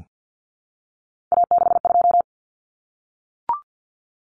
Morse code, heard as a keyed tone, about 50 wpm, 700 Hz. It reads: A62
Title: Common Cut Numbers - 3-Digit 50wpm